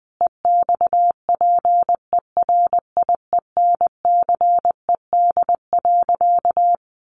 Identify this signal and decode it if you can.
Morse code, heard as a keyed tone, about 20 wpm, 700 Hz. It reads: EXPERIENCED.
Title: Sentences from Top 1100 Words 20wpm